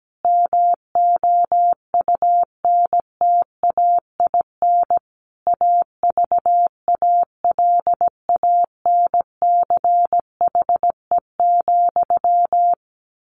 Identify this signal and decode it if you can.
Morse code, heard as a keyed tone, about 17 wpm, 700 Hz. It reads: MOUNTAIN AVALANCHE,
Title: A Sound Of Thunder 17wpm